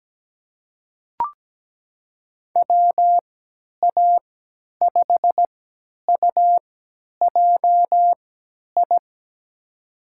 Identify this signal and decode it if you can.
Morse code, heard as a keyed tone, about 17 wpm, spaced as for 10 wpm, 700 Hz. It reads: WA5UJI
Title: USA Call Signs - 6 Characters - 3x Character Spacing - 17wpm